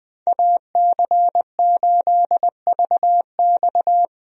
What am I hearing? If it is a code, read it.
Morse code, heard as a keyed tone, about 20 wpm, 700 Hz. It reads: AC8VX
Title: USA Call Signs - 5 Characters - 20wpm